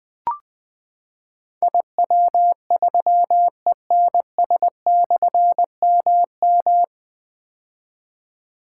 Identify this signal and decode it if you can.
Morse code, heard as a keyed tone, about 20 wpm, 700 Hz. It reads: IW3ENS/MM
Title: Difficult Call Signs 20wpm